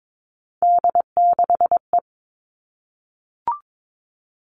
Morse code, heard as a keyed tone, about 22 wpm, 700 Hz. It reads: D6E
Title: All Cut Numbers - 3-Digit 22wpm